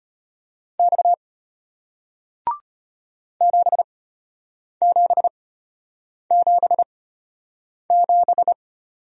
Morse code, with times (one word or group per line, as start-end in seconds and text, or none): 0.8-1.1 X
3.4-8.5 7777